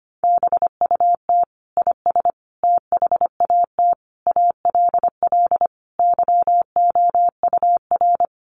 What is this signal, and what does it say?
Morse code, heard as a keyed tone, about 25 wpm, 700 Hz. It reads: BUT IS THAT ALL YOUR
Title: Sentences from Top 1300 Words 25wpm